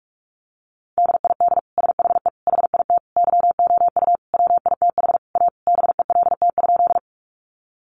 Morse code, heard as a keyed tone, about 44 wpm, 700 Hz. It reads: DID SHE HIT YOU WITH A BELT?
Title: Sentences from Top 1400 Words 45wpm